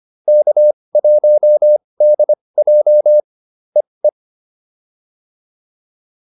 Morse code, heard as a keyed tone, about 25 wpm, spaced as for 20 wpm, 590 Hz. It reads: K1DJ EE